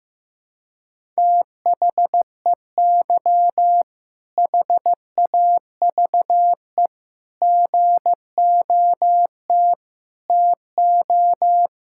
Morse code, heard as a keyed tone, about 16 wpm, 700 Hz. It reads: THEY HAVE GOT TO
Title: Sentences from Top 1000 Words - Review Part 6 - 15wpm